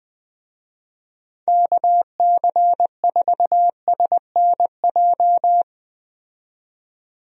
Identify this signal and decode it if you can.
Morse code, heard as a keyed tone, about 20 wpm, 700 Hz. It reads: KC4SNJ
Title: USA Call Signs - 6 Characters - 20wpm